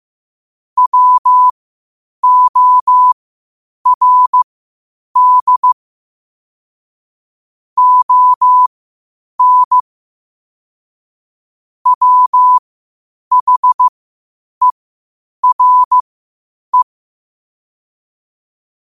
Morse code, heard as a keyed tone, about 15 wpm, spaced as for 8 wpm, 1000 Hz. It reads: WORD ON WHERE
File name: qotdfiles / 8wpm